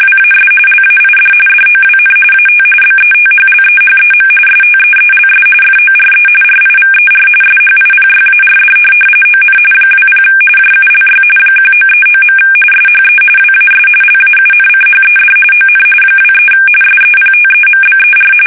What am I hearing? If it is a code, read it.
RTTY, about 45.45 baud. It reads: WAS LAID DOWN AT BETHLEHEM STEELS FORE RIVER SHIPYARD ON 20 JULY 1939. SHE WAS LAUNCHED ON 23 SEPTEMBER
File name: fsk850 / USS-Massachusetts